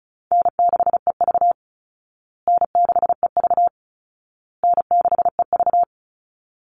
Morse code, heard as a keyed tone, about 34 wpm, 700 Hz. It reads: N6EV N6EV N6EV
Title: Easy Call Signs Repeated 3x - 35wpm